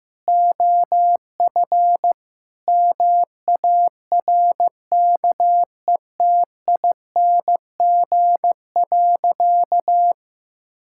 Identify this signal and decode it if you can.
Morse code, heard as a keyed tone, about 16 wpm, 700 Hz. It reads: OF MARKETING.
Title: Sentences from Top 1000 Words - Review Part 6 - 15wpm